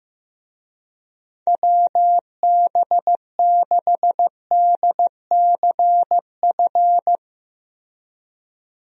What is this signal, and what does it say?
Morse code, heard as a keyed tone, about 16 wpm, 700 Hz. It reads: WB6DCF
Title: USA Call Signs - 6 Characters - 15wpm